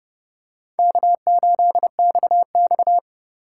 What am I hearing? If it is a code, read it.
Morse code, heard as a keyed tone, about 30 wpm, 700 Hz. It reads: K8XX
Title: USA Call Signs - 4 Characters - 6x Character Spacing - 30wpm